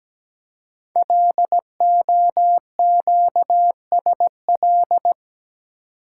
Morse code, heard as a keyed tone, about 17 wpm, 700 Hz. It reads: LOQSL
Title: Code Groups - 5 Letters - 17wpm